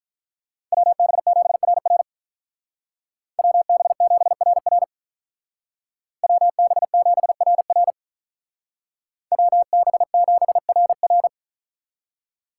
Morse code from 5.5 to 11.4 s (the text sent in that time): WB7RR WB7RR